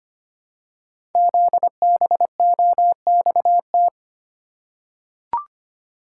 Morse code, heard as a keyed tone, about 25 wpm, 700 Hz. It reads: ZBOXT
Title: Code Groups - 5 Letters - 25wpm